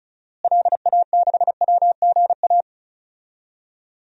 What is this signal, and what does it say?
Morse code, heard as a keyed tone, about 34 wpm, 700 Hz. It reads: LA6WGA
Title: DXCC-100 Call Signs - 6 Characters - Repeated 2x - 35wpm